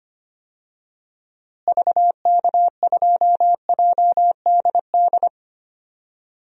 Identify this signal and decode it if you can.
Morse code, heard as a keyed tone, about 25 wpm, 700 Hz. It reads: VK2JDD
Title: DXCC-100 Call Signs - 6 Characters - Repeated 2x - 25wpm